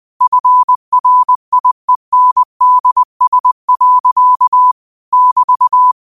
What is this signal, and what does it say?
Morse code, heard as a keyed tone, about 20 wpm, 1000 Hz. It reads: FRIENDS. <BT>
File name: qotdfiles / 20wpm